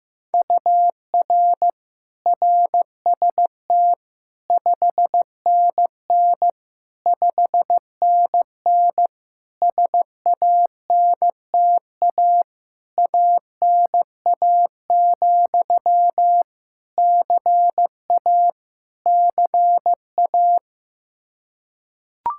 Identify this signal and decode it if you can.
Morse code, heard as a keyed tone, about 16 wpm, 700 Hz. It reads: UR RST 5NN 5NN SANTA ANA, CA CA
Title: Practice QSO - Exchange 1 - Beginning - 15wpm